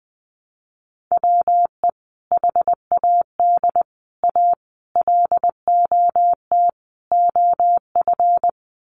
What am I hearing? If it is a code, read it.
Morse code, heard as a keyed tone, about 20 wpm, 700 Hz. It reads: WE HAD A LOT OF
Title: Sets of 5 Words Encore 20wpm